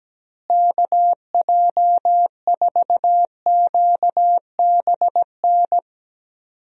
Morse code, heard as a keyed tone, about 17 wpm, 700 Hz. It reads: KJ4QBN
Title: USA Call Signs - 6 Characters - Repeated 2x - 17wpm